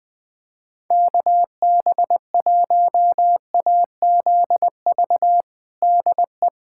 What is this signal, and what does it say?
Morse code, heard as a keyed tone, about 21 wpm, 700 Hz. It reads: KB1AZV DE
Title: USA Callsign DE Callsign - 6 Characters - 20wpm